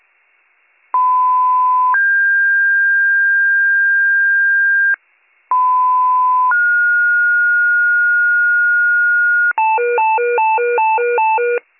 Two-tone page, heard as a one-second tone, then a three-second tone, about 1010 then 1600 Hz, about 1010 then 1430 Hz, with noise. It is heard from 0.9 to 9.5 s.